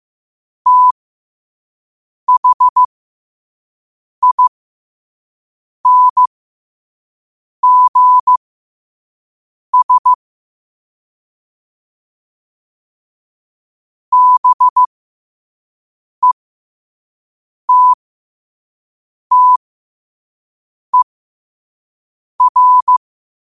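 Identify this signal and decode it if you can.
Morse code, heard as a keyed tone, about 15 wpm, spaced as for 5 wpm, 1000 Hz. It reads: THINGS BETTER
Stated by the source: Morse quote of the day at 5 wpm.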